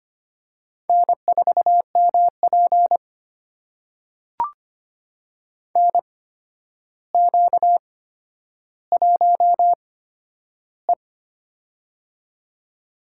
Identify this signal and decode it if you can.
Morse code, heard as a keyed tone, about 25 wpm, spaced as for 11 wpm, 700 Hz. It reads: N4MP NQ1E
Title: USA Call Signs - 4 Characters - 8x Character Spacing - 25wpm